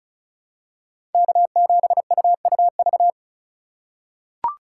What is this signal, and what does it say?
Morse code, heard as a keyed tone, about 34 wpm, 700 Hz. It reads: K7UUV
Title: USA Call Signs - 2x Character Spacing - 35wpm